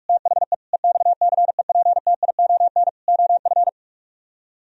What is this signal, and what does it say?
Morse code, heard as a keyed tone, about 50 wpm, 700 Hz. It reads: THE EXCEPTION OF